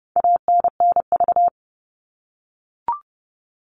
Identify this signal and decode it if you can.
Morse code, heard as a keyed tone, about 28 wpm, 700 Hz. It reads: ANNV
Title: All Cut Numbers - 4-Digit 30wpm